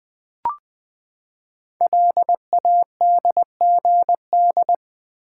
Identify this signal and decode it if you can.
Morse code, heard as a keyed tone, about 20 wpm, 700 Hz. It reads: LADGD
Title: Code Groups - 5 Letters - 20wpm